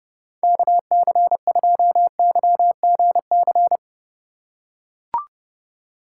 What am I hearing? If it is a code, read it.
Morse code, heard as a keyed tone, about 32 wpm, 700 Hz. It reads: KC2YGC
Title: USA Call Signs - 6 Characters - 2x Character Spacing - 30wpm